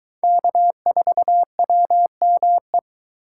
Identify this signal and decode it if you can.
Morse code, heard as a keyed tone, about 22 wpm, 700 Hz. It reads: K4WME